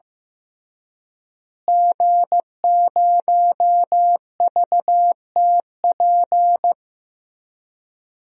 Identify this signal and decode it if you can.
Morse code, heard as a keyed tone, about 15 wpm, 700 Hz. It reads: G0VTP
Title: DXCC-100 Call Signs - 5 Characters - Repeated 3x - 15wpm